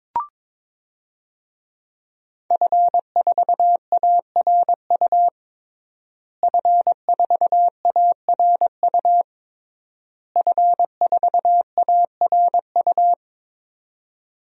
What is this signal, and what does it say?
Morse code, heard as a keyed tone, about 22 wpm, 700 Hz. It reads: F4ARU F4ARU F4ARU
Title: DXCC-100 Call Signs - 5 Characters - Repeated 3x - 22wpm